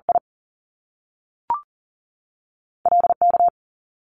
Morse code, heard as a keyed tone, about 41 wpm, spaced as for 6 wpm, 700 Hz. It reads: ILK
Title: Top Two Letter Combinations in Words 40wpm